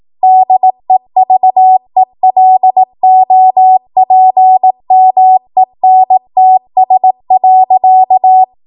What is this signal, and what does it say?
Morse code, heard as a keyed tone, about 18 wpm, 750 Hz. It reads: DEVELOPMENTS.
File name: Morse / 18WPM